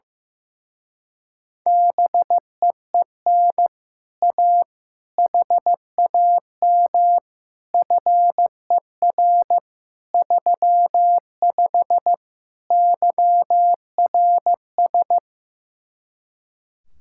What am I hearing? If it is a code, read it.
Morse code, heard as a keyed tone, about 15 wpm, 700 Hz. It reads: BEEN A HAM FER 35 YRS
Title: Speed-Racing Common QSO Phrases Encore - 15wpm